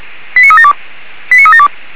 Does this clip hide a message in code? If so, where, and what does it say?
Yes: five-tone selective call (ZVEI) from 0.4 to 1.7 s, heard as quick runs of five tones, tones of about 80 ms. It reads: 89372 89372